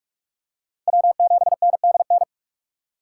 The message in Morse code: W7NDN

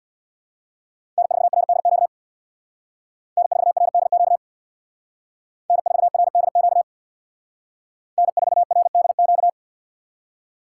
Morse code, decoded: N4RDQ N4RDQ N4RDQ N4RDQ

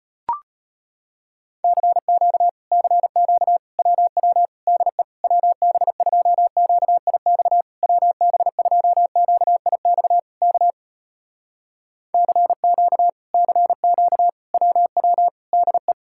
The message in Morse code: CQ CQ WW DE WB2QIX WB2QIX K CQ CQ WW DE

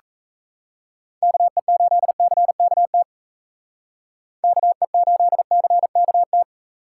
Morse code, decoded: KE8CKT KE8CKT

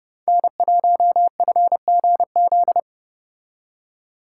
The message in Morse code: N1FGZ

